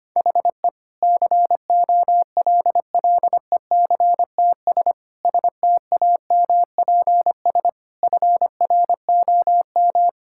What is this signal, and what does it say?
Morse code, heard as a keyed tone, about 25 wpm, 700 Hz. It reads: HE COLLECTS STAMPS FROM
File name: practice / 25wpm / Difficult Sentences with Hints - 25wpm